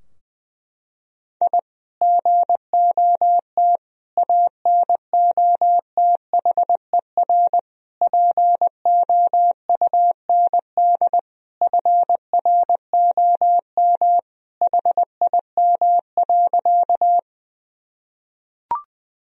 Morse code, heard as a keyed tone, about 20 wpm, 700 Hz. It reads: I GOT ANOTHER POUND FROM HIM.
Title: Sentences from Top 1000 Words - Review Part 2 - 20wpm